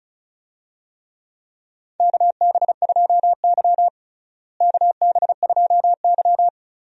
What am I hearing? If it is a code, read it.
Morse code, heard as a keyed tone, about 36 wpm, 700 Hz. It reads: KB2Y KB2Y
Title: USA Call Signs - 4 Characters - Repeated 3x - 35wpm